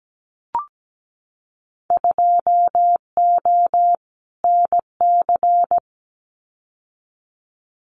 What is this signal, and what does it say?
Morse code, heard as a keyed tone, about 17 wpm, 700 Hz. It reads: 2O NC
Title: Winter Field Day Exchanges 17wpm